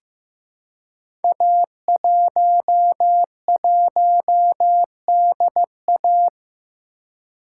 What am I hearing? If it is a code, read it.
Morse code, heard as a keyed tone, about 15 wpm, 700 Hz. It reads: A11DA
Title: Code Groups - 5 Letter:Numbers - 15wpm